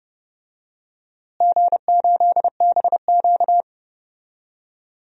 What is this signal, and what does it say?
Morse code, heard as a keyed tone, about 30 wpm, 700 Hz. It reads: G8BQ
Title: DXCC-100 Call Signs - 4 Characters - Repeated 2x - 30wpm